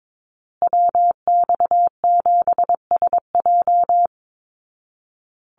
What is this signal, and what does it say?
Morse code, heard as a keyed tone, about 22 wpm, 700 Hz. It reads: WX7SJ
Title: Call Signs Repeated 2x - 22wpm